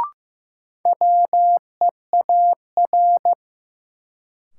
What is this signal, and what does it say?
Morse code, heard as a keyed tone, about 16 wpm, 700 Hz. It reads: WEAR
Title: Sentences from Top 1500 Words - New Word Drill Alt - 15wpm